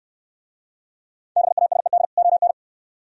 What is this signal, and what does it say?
Morse code, heard as a keyed tone, about 64 wpm, 690 Hz. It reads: BASED ON